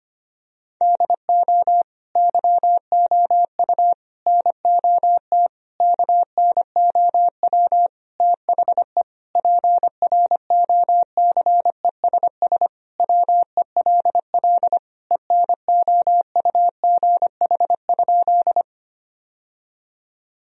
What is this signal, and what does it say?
Morse code, heard as a keyed tone, about 25 wpm, 700 Hz. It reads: DO YOU NOT KNOW THE PROCESS WELL ENOUGH?
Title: Sentences from Top 1000 Words - Review Part 3 - 25wpm